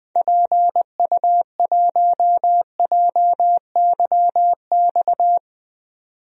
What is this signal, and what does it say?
Morse code, heard as a keyed tone, about 20 wpm, 700 Hz. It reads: PU1JYX
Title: DXCC-100 Call Signs - 6 Characters - 20wpm